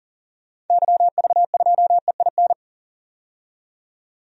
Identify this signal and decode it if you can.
Morse code, heard as a keyed tone, about 40 wpm, 700 Hz. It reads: YV2EIN
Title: DXCC-100 Call Signs - 6 Characters - 40wpm